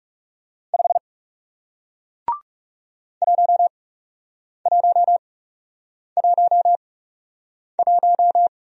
Morse code, heard as a keyed tone, about 40 wpm, spaced as for 8 wpm, 700 Hz. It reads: 5 1111